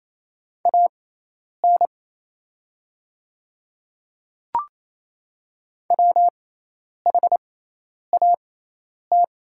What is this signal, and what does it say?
Morse code, heard as a keyed tone, about 28 wpm, spaced as for 10 wpm, 700 Hz. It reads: AN WHAT
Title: 6x Character Spacing - Top 100 Words - 28wpm